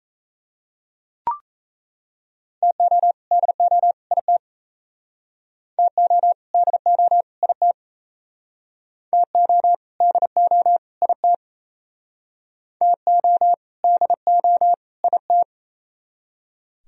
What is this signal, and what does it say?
Morse code, heard as a keyed tone, about 37 wpm, 700 Hz. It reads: TO DO IT TO DO IT TO DO IT TO DO IT